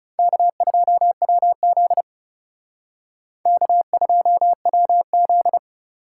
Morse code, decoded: K2WZ K2WZ

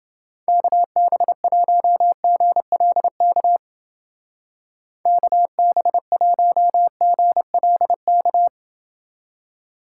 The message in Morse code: KB1GLK KB1GLK

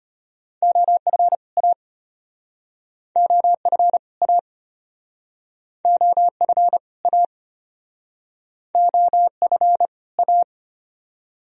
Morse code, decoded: OF A OF A OF A OF A

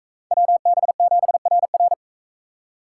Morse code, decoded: WB7RR

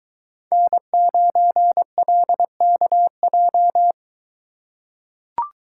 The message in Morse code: N9LKJ

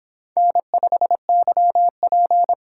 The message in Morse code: N5YP

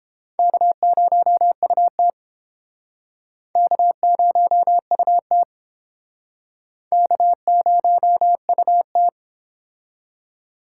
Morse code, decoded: K0UT K0UT K0UT

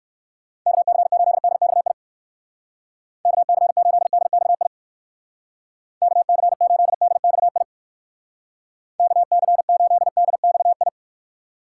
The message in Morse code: KC8DXI KC8DXI KC8DXI KC8DXI

